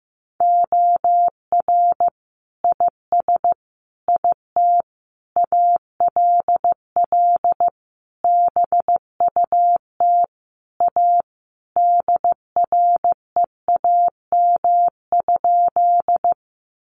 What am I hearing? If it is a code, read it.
Morse code, heard as a keyed tone, about 15 wpm, 700 Hz. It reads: OR IS IT ALL BUT A DREAM?
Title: Sentences from Top 1500 Words 15wpm